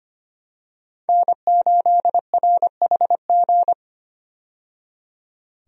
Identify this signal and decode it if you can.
Morse code, heard as a keyed tone, about 25 wpm, 700 Hz. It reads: N8RHG
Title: USA Call Signs - 5 Characters - 25wpm